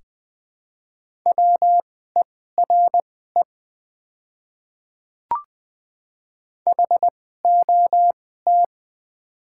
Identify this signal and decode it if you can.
Morse code, heard as a keyed tone, about 20 wpm, spaced as for 15 wpm, 700 Hz. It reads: WERE HOT
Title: 2x Character Spacing - Top 500 Words - 20wpm